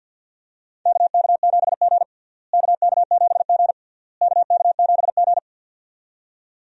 Morse code, heard as a keyed tone, about 49 wpm, 700 Hz. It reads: KK7G KK7G KK7G
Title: USA Call Signs - 4 Characters - Repeated 3x - 50wpm